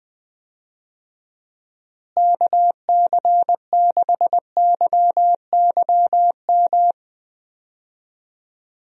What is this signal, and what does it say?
Morse code, heard as a keyed tone, about 20 wpm, 700 Hz. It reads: KC6YYM
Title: USA Call Signs - 6 Characters - Repeated 2x - 20wpm